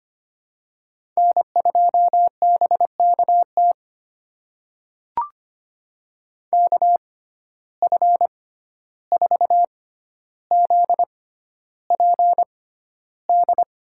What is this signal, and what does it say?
Morse code, heard as a keyed tone, about 25 wpm, spaced as for 12 wpm, 700 Hz. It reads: N2BKT KF4ZPD
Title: USA Call Signs - 6x Character Spacing - 25wpm